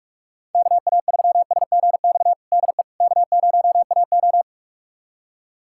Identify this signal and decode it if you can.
Morse code, heard as a keyed tone, about 46 wpm, 690 Hz. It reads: KA3SGX DE K0AO
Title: USA Callsign DE Callsign 45wpm